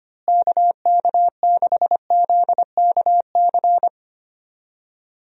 Morse code, heard as a keyed tone, about 25 wpm, 700 Hz. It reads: KK6ZKC
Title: USA Call Signs - 6 Characters - Repeated 3x - 25wpm